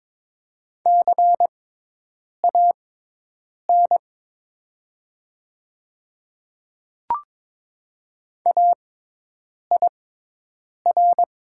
Morse code, heard as a keyed tone, about 23 wpm, spaced as for 8 wpm, 700 Hz. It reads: CAN AIR
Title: 6x Character Spacing - Top 200 Words - 22wpm